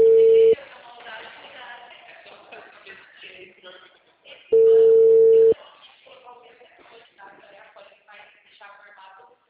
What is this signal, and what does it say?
Ringback tone (China, 450 Hz), with speech.
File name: out- -2011- - - .260